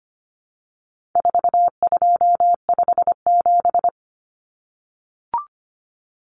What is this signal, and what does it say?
Morse code, heard as a keyed tone, about 25 wpm, 700 Hz. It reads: 4257
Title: Numbers - 4-Digit 25wpm